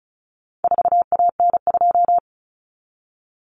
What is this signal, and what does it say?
Morse code, heard as a keyed tone, about 34 wpm, 700 Hz. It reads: 4AN2
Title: Common Cut Numbers - 4-Digit 35wpm